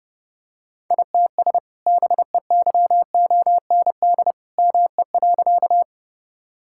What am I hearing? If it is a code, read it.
Morse code, heard as a keyed tone, about 30 wpm, 700 Hz. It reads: ITS BEYOND ME.
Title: Sentences from Top 1000 Words - Review Part 6 - 30wpm